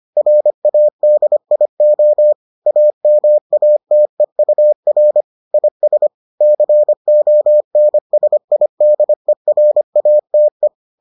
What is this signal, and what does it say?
Morse code, heard as a keyed tone, about 25 wpm, 600 Hz. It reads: RADIO AMATEUR IS CONSIDERATE